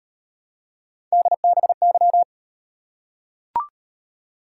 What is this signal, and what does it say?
Morse code, heard as a keyed tone, about 38 wpm, 700 Hz. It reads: DBY